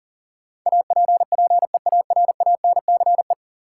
Morse code, heard as a keyed tone, about 37 wpm, 700 Hz. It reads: APPEARANCE
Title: Sentences from Top 1100 Words - New Word Drill Alt - 40wpm